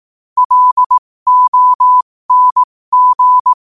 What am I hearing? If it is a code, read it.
Morse code, heard as a keyed tone, about 18 wpm, spaced as for 15 wpm, 1000 Hz. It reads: LONG
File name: qotdfiles / 15wpm